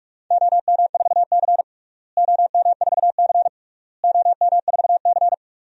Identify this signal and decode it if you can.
Morse code, heard as a keyed tone, about 41 wpm, 700 Hz. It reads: OM4C OM4C OM4C
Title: DXCC-100 Call Signs - 4 Characters - Repeated 3x - 45wpm